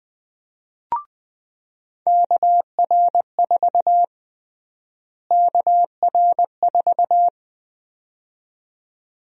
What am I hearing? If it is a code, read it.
Morse code, heard as a keyed tone, about 21 wpm, 700 Hz. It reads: KR4 KR4
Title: USA Call Sign Prefixes - Repeated 2x - 20wpm